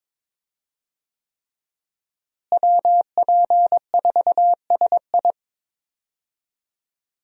Morse code, heard as a keyed tone, about 21 wpm, 700 Hz. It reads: WP4SI
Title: USA Call Signs - 5 Characters - Repeated 3x - 22wpm